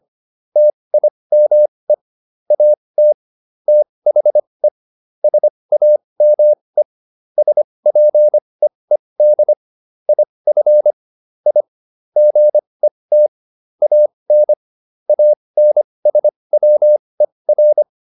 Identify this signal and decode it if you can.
Morse code, heard as a keyed tone, about 25 wpm, spaced as for 20 wpm, 600 Hz. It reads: TIME AT THE SAME SPEED IF I GET AN ANSWER